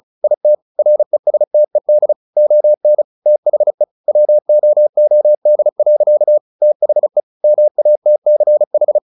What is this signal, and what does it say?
Morse code, heard as a keyed tone, about 35 wpm, 600 Hz. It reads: IT RESTED ON THE WOOD. THE MATCH
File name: the old man and the sea 2 35wpm